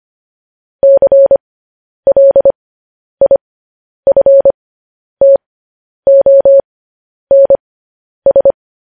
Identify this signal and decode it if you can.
Morse code, heard as a keyed tone, about 25 wpm, spaced as for 10 wpm, 550 Hz. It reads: CLIFTONS
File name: podcast / News headlines-1x-25x10